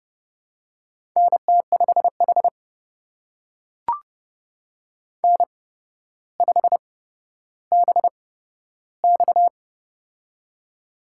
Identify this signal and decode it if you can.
Morse code, heard as a keyed tone, about 30 wpm, spaced as for 13 wpm, 700 Hz. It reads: NT5H N5BX
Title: USA Call Signs - 4 Characters - 8x Character Spacing - 30wpm